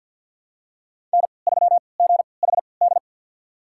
Morse code, heard as a keyed tone, about 47 wpm, spaced as for 28 wpm, 690 Hz. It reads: N3GHD
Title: USA Call Signs - 5 Characters - 3x Character Spacing - 50wpm